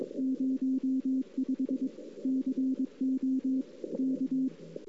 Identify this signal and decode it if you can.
Morse code, heard as a keyed tone, about 22 wpm, 260 Hz. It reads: 05 CO K
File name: CW / 2008-ss-cw-k5tr-op-n5rz -03:00